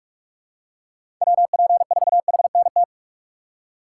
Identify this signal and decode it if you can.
Morse code, heard as a keyed tone, about 47 wpm, 700 Hz. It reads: WP4HNT
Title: USA Call Signs - 6 Characters - 3x Character Spacing - 45wpm